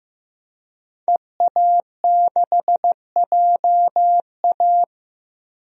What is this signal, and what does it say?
Morse code, heard as a keyed tone, about 15 wpm, 700 Hz. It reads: EA6JA
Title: DXCC-100 Call Signs - 5 Characters - Repeated 2x - 15wpm